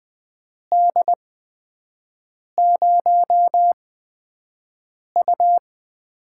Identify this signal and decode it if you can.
Morse code, heard as a keyed tone, about 20 wpm, spaced as for 5 wpm, 700 Hz. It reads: D0U
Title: USA Call Signs - 6 Characters - 8x Character Spacing - 20wpm